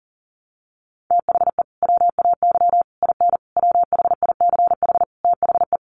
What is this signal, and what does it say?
Morse code, heard as a keyed tone, about 40 wpm, 700 Hz. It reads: THE WAY IN WHICH THE
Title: Sets of 5 Words Encore2 40wpm